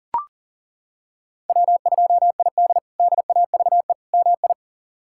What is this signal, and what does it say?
Morse code, heard as a keyed tone, about 40 wpm, 700 Hz. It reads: W2ID DAVE MI
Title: K1USN SST QSO Practice - 40wpm